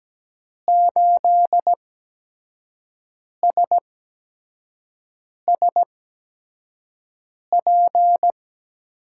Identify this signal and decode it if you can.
Morse code, heard as a keyed tone, about 17 wpm, spaced as for 5 wpm, 700 Hz. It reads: 8SSP
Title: USA Call Signs - 6 Characters - 8x Character Spacing - 17wpm